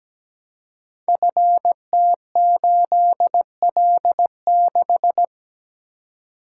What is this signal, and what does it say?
Morse code, heard as a keyed tone, about 17 wpm, 700 Hz. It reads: FT8L6
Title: Code Groups - 5 Letter:Numbers - 17wpm